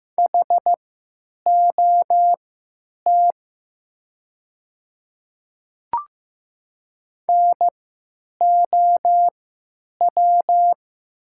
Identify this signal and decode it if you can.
Morse code, heard as a keyed tone, about 15 wpm, spaced as for 9 wpm, 700 Hz. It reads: HOT NOW
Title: 3x Character Spacing - Top 100 Words - 15wpm